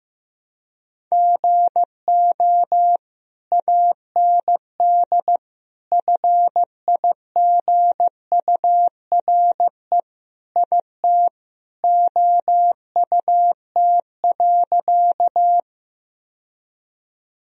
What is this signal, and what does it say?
Morse code, heard as a keyed tone, about 15 wpm, 700 Hz. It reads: GO AND FIGURE IT OUT.
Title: Sentences from Top 1000 Words - Review Part 2 - 15wpm